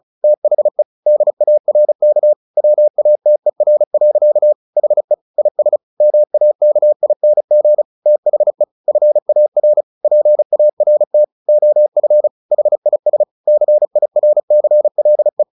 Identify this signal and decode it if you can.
Morse code, heard as a keyed tone, about 34 wpm, 600 Hz. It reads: THE DARK WATER. HE IS MAKING THE FAR PART OF HIS CIRCLE
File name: the old man and the sea 2 35wpm